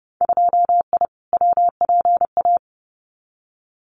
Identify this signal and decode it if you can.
Morse code, heard as a keyed tone, about 30 wpm, 700 Hz. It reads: 2I WPA
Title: Winter Field Day Exchanges 30wpm